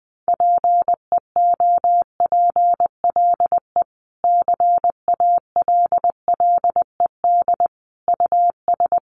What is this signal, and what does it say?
Morse code, heard as a keyed tone, about 19 wpm, 700 Hz. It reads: PEOPLE CALLED US
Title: A Sound Of Thunder 20wpm